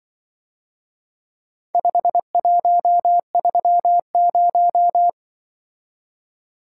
Morse code, heard as a keyed tone, about 24 wpm, 700 Hz. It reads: 5130
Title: Speed-Racing - Four Letter-Number Combinations - 20wpm